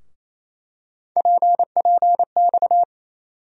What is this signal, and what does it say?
Morse code, heard as a keyed tone, about 28 wpm, 700 Hz. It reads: PPX
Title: USA Call Sign Postfixes - 3 Letters 28wpm